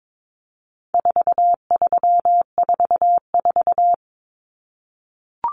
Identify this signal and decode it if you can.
Morse code, heard as a keyed tone, about 22 wpm, 700 Hz. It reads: 4344
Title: Common Cut Numbers - 4-Digit 22wpm